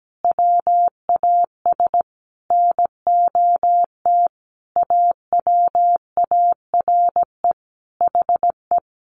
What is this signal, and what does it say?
Morse code, heard as a keyed tone, about 17 wpm, 700 Hz. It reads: WAS NOT AWARE HE
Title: Sentences from Top 1200 Words 17wpm